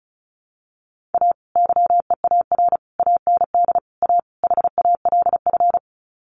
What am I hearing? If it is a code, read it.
Morse code, heard as a keyed tone, about 34 wpm, 700 Hz. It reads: A YEAR AND A HALF
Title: Sets of 5 Words 35wpm